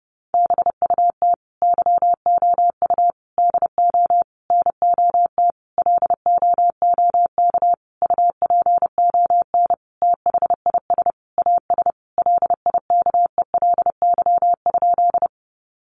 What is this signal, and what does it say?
Morse code, heard as a keyed tone, about 28 wpm, 700 Hz. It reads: BUT YOU DO NOT LOOK UPON THIS AS LIKELY?
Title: Sentences from Top 1100 Words 30wpm